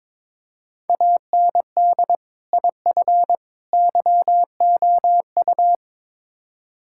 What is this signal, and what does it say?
Morse code, heard as a keyed tone, about 22 wpm, 700 Hz. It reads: AND IF YOU
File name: practice / 22wpm / Build A Phrase - 5-Word Phrases - 22wpm